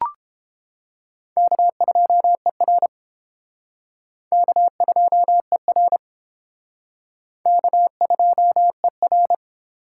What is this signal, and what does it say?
Morse code, heard as a keyed tone, about 30 wpm, 700 Hz. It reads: K2ER K2ER K2ER